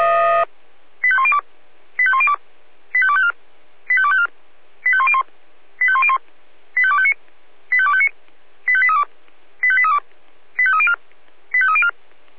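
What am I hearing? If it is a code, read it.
Five-tone selective call (ZVEI), heard as quick runs of five tones, tones of about 70 ms. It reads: 85102 85102 85254 85254 85101 85101 85259 85259 85832 85832 85304 85304
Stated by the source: ZVEI five-tone selective call received by SDR.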